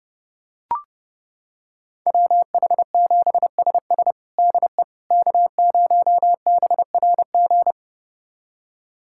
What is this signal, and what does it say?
Morse code, heard as a keyed tone, about 32 wpm, 700 Hz. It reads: WH7SS DE K0BRG
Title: USA Callsign DE Callsign - 5 Characters - 30wpm